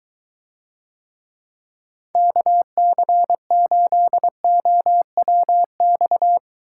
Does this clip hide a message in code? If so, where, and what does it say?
Yes: Morse code from 2.2 to 6.4 s, heard as a keyed tone, about 23 wpm, 700 Hz. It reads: KC8OWX